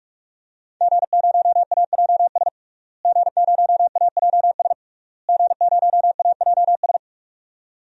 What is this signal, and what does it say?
Morse code, heard as a keyed tone, about 42 wpm, 700 Hz. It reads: G0AJS G0AJS G0AJS
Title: DXCC-100 Call Signs - 5 Characters - Repeated 3x - 45wpm